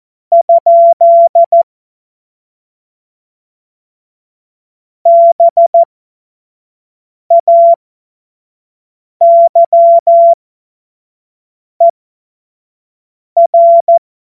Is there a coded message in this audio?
Yes: Morse code from 0.3 to 14.0 s, heard as a keyed tone, about 14 wpm, spaced as for 5 wpm, 680 Hz. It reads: ? BAYER